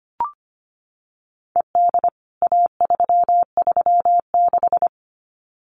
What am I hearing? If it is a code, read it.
Morse code, heard as a keyed tone, about 25 wpm, 700 Hz. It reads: ED A336
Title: CWT Exchange 25wpm